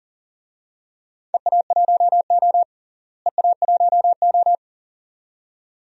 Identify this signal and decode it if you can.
Morse code, heard as a keyed tone, about 40 wpm, 700 Hz. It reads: EA1O EA1O
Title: DXCC-100 Call Signs - 4 Characters - Repeated 2x - 40wpm